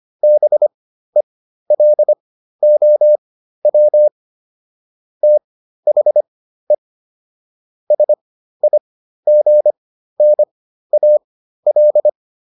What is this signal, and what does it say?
Morse code, heard as a keyed tone, about 25 wpm, spaced as for 13 wpm, 600 Hz. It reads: BELOW THE SIGNAL